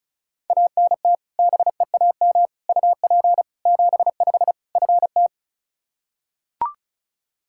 Morse code, heard as a keyed tone, about 34 wpm, 700 Hz. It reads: ANT BEAM UP 75 FT
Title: Practice QSO - Exchange 2 - Info - 35wpm